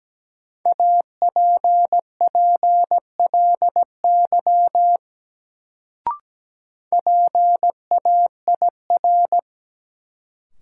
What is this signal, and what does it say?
Morse code, heard as a keyed tone, about 17 wpm, 700 Hz. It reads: APPLY PAIR
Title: Sentences from Top 1200 Words - New Word Drill Alt - 17wpm